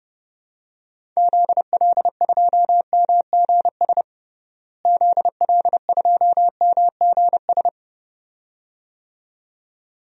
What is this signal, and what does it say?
Morse code, heard as a keyed tone, about 32 wpm, 700 Hz. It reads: ZL2MGS ZL2MGS
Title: DXCC-100 Call Signs - 6 Characters - Repeated 2x - 30wpm